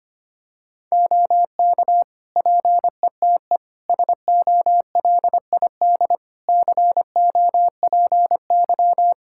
Morse code, heard as a keyed tone, about 25 wpm, 700 Hz. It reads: OK PETE SOLID COPY
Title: Practice QSO - Exchange 3 - More Info - 25wpm